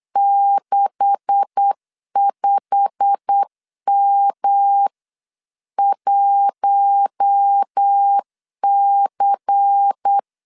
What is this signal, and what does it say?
Morse code, heard as a keyed tone, about 8 wpm, 800 Hz. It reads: 65M1C
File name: CW / DET / 1 9 1500C S B 8wpm 5624